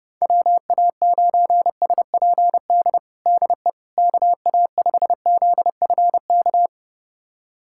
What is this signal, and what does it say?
Morse code, heard as a keyed tone, about 30 wpm, 700 Hz. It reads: WA9SPD DE KA5ZFK
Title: USA Callsign DE Callsign - 6 Characters - 30wpm